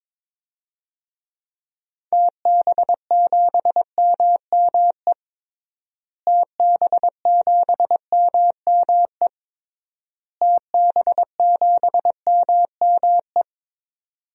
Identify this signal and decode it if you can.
Morse code, heard as a keyed tone, about 22 wpm, 700 Hz. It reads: TB7MME TB7MME TB7MME
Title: DXCC-100 Call Signs - 6 Characters - Repeated 3x - 22wpm